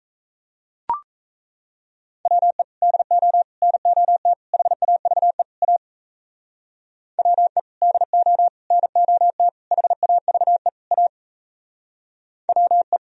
Morse code, heard as a keyed tone, about 37 wpm, 700 Hz. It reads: WE DO NOT HAVE A WE DO NOT HAVE A WE